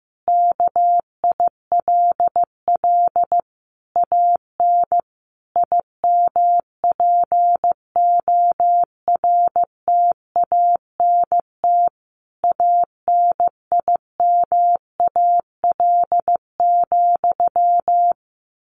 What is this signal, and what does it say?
Morse code, heard as a keyed tone, about 15 wpm, 700 Hz. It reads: KILL AN IMPORTANT ANIMAL,
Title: A Sound Of Thunder 15wpm